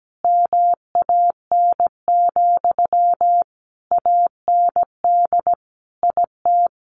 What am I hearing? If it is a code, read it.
Morse code, heard as a keyed tone, about 17 wpm, 700 Hz. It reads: MAN, AND IT
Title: Sentences from Top 200 Words 17wpm